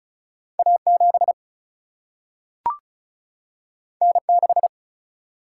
Morse code, heard as a keyed tone, about 36 wpm, 700 Hz. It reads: A7 N6
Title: Single-Letter Single-Number 35wpm